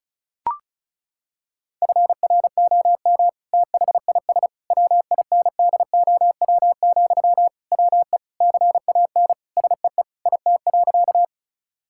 Morse code, heard as a keyed tone, about 36 wpm, 700 Hz. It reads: FROM THIS WINDOW, WE CAN SEE IT.
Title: Sentences from Top 1600 Words 35wpm